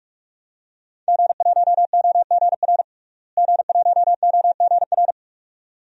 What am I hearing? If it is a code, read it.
Morse code, heard as a keyed tone, about 46 wpm, 700 Hz. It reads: G1OGR G1OGR
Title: DXCC-100 Call Signs - 5 Characters - Repeated 2x - 45wpm